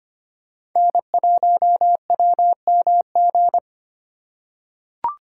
Morse code, heard as a keyed tone, about 25 wpm, 700 Hz. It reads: N1WMG
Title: USA Call Signs - 3x Character Spacing - 25wpm